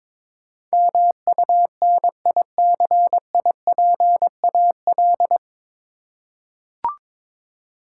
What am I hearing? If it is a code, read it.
Morse code, heard as a keyed tone, about 22 wpm, 700 Hz. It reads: MUNICIPAL
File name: 9 Letter Words - First 3 Letters Given - 22wpm